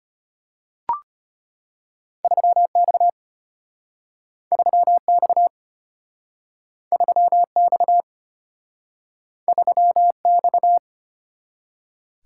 Morse code, heard as a keyed tone, about 34 wpm, 700 Hz. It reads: 3X 3X 3X 3X